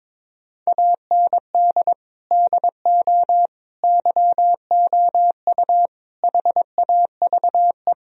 Morse code, heard as a keyed tone, about 21 wpm, 700 Hz. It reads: AND DO YOU HAVE
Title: Sentences from Top 1000 Words - Review Part 3 - 22wpm